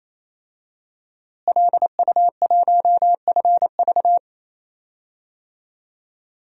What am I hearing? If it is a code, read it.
Morse code, heard as a keyed tone, about 28 wpm, 700 Hz. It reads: LU1FV
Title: DXCC-100 Call Signs - 5 Characters - Repeated 3x - 28wpm